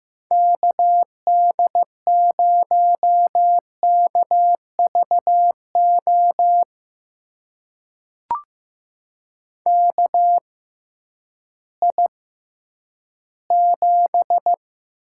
Morse code, heard as a keyed tone, about 15 wpm, spaced as for 10 wpm, 700 Hz. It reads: KD0KVO KI7
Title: USA Call Signs - 6 Characters - 6x Character Spacing - 15wpm